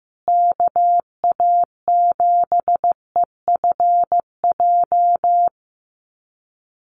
Morse code, heard as a keyed tone, about 14 wpm, 700 Hz. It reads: KA7EFJ
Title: USA Call Signs Repeated 2x 15wpm